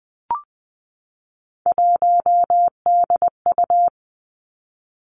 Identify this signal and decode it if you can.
Morse code, heard as a keyed tone, about 20 wpm, 700 Hz. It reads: 1DU
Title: Three Challenging Letter-Number Combinations 20wpm